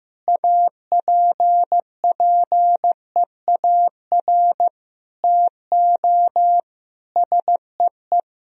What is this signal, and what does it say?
Morse code, heard as a keyed tone, about 15 wpm, 700 Hz. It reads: APPEAR TO SEE
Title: Sentences from Top 1000 Words - Review Part 2 - 15wpm